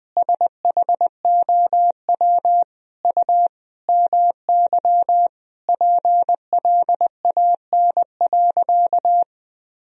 Morse code, heard as a keyed tone, about 20 wpm, 700 Hz. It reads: SHOW U MY PLAN.
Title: Sentences from Top 500 Words and Abbreviations 20wpm